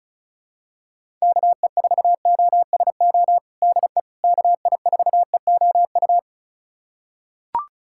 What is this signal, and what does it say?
Morse code, heard as a keyed tone, about 34 wpm, 700 Hz. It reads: KE4OSO DE KI4EOU
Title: USA Callsign DE Callsign - 6 Characters - 35wpm